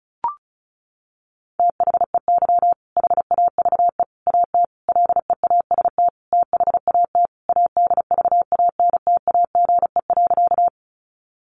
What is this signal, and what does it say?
Morse code, heard as a keyed tone, about 34 wpm, 700 Hz. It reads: THEY HAVE AT LEAST THAT ADVANTAGE.
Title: Sentences from Top 900 Words 35wpm